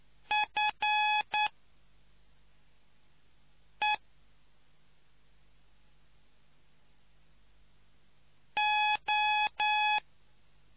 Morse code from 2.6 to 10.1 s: E O